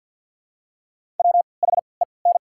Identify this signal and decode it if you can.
Morse code, heard as a keyed tone, about 54 wpm, spaced as for 29 wpm, 700 Hz. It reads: WHEN
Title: 3x Character Spacing - Top 100 Words - 50wpm